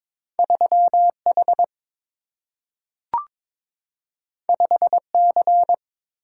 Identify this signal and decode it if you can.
Morse code, heard as a keyed tone, about 22 wpm, 700 Hz. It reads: 3 H 5 C
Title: Single-Letter Single-Number 22wpm